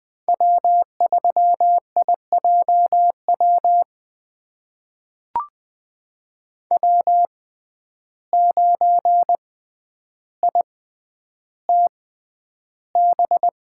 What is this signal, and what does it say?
Morse code, heard as a keyed tone, about 20 wpm, spaced as for 10 wpm, 700 Hz. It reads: W3IJW W9ITB
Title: USA Call Signs - 5 Characters - 6x Character Spacing - 20wpm